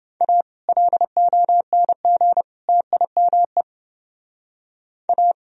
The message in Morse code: A LONG TIME A

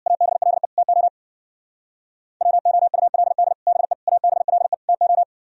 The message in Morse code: ABLE TO WOULD BE ABLE TO

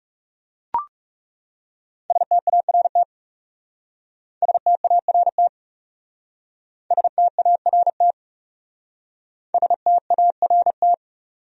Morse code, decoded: START START START START

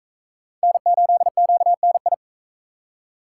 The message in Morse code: N8QNI

